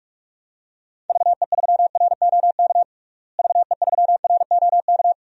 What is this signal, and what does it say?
Morse code, heard as a keyed tone, about 46 wpm, 700 Hz. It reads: VE3ROK VE3ROK
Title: DXCC-100 Call Signs - 6 Characters - Repeated 2x - 45wpm